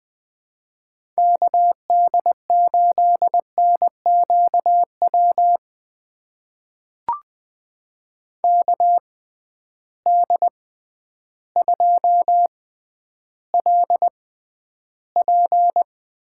Morse code, heard as a keyed tone, about 21 wpm, spaced as for 11 wpm, 700 Hz. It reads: KD8NQW KD2LP
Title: USA Call Signs - 6 Characters - 6x Character Spacing - 20wpm